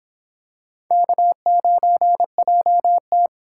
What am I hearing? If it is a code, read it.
Morse code, heard as a keyed tone, about 26 wpm, 700 Hz. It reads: K9JT